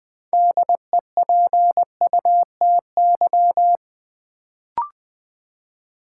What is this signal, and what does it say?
Morse code, heard as a keyed tone, about 20 wpm, 700 Hz. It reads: DEPUTY
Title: Build A Word - 3 Letters At A Time - 6-Letter Words - 20wpm